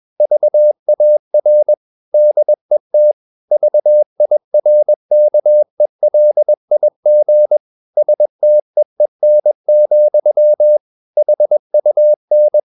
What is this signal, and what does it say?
Morse code, heard as a keyed tone, about 21 wpm, 600 Hz. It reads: VAR DET VIRKELIG STEEN, HUN
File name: Ferie QRS 21wpm